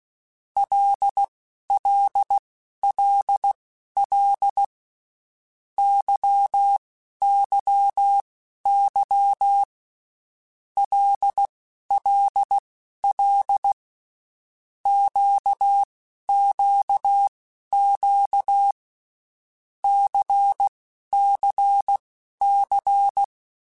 Morse code, heard as a keyed tone, about 16 wpm, spaced as for 11 wpm, 800 Hz. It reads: LLLL YYY LLL QQQ CCC
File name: Groups@10 / Groups@10 mp3s / LFCYQ@10